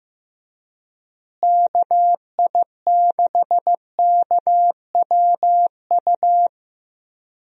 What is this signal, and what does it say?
Morse code, heard as a keyed tone, about 15 wpm, 700 Hz. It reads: KI6KWU
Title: USA Call Signs - 6 Characters - Repeated 2x - 15wpm